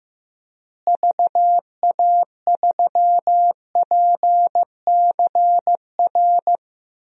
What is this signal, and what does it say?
Morse code, heard as a keyed tone, about 15 wpm, 700 Hz. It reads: VA3PCR
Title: DXCC-100 Call Signs - 6 Characters - Repeated 3x - 15wpm